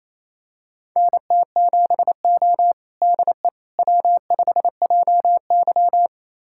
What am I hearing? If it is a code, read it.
Morse code, heard as a keyed tone, about 28 wpm, 700 Hz. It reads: NT7O DE W5JY
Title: USA Callsign DE Callsign - 4 Characters - 28wpm